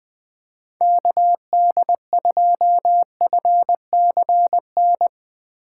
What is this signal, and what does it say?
Morse code, heard as a keyed tone, about 20 wpm, 700 Hz. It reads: KD2FCN
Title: USA Call Signs - 6x Character Spacing - 20wpm